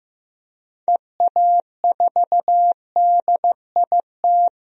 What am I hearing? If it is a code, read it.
Morse code, heard as a keyed tone, about 15 wpm, 700 Hz. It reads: EA4DIT
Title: DXCC-100 Call Signs - 6 Characters - 15wpm